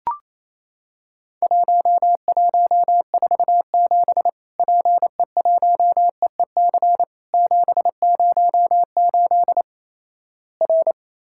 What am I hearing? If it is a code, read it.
Morse code, heard as a keyed tone, about 28 wpm, 700 Hz. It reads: 1147 PE1EEC 708 R
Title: Worked All Europe DX Contest - QTC Exchange - 28wpm